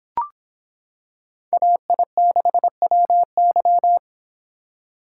Morse code, heard as a keyed tone, about 26 wpm, 700 Hz. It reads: AI6WY